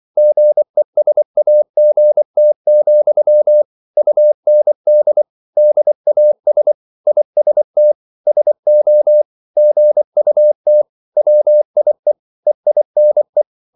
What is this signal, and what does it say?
Morse code, heard as a keyed tone, about 24 wpm, 600 Hz. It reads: GESAGT, UND DAS IST SO GUT WIE EINE